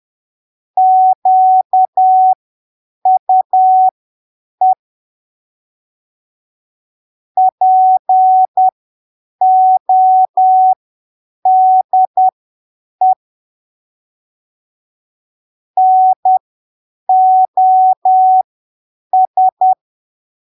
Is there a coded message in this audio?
Yes: Morse code from 0.8 to 19.7 s, heard as a keyed tone, about 10 wpm, spaced as for 7 wpm, 750 Hz. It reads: QUE PODE NOS